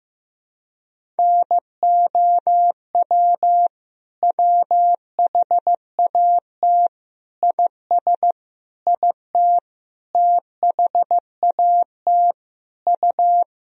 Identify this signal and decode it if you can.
Morse code, heard as a keyed tone, about 16 wpm, 700 Hz. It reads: NOW WHAT IS IT THAT U
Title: Sentences from Top 200 Words and Abbreviations 15wpm